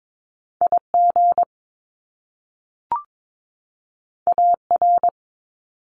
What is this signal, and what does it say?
Morse code, heard as a keyed tone, about 22 wpm, 700 Hz. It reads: IG AR
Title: Top Two Letter Combinations in Words 22wpm